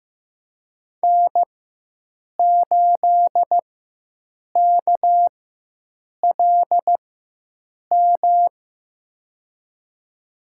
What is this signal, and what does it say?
Morse code, heard as a keyed tone, about 15 wpm, spaced as for 7 wpm, 700 Hz. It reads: N8KLM
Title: USA Call Signs - 5 Characters - 4x Character Spacing - 15wpm